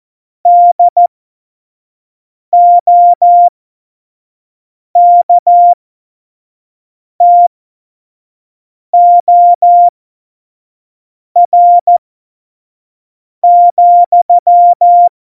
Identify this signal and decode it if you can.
Morse code, heard as a keyed tone, about 14 wpm, spaced as for 5 wpm, 700 Hz. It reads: DOKTOR,